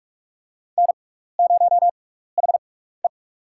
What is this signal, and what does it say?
Morse code, heard as a keyed tone, about 46 wpm, spaced as for 16 wpm, 700 Hz. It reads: N0HE
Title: USA Call Signs - 4 Characters - 6x Character Spacing - 45wpm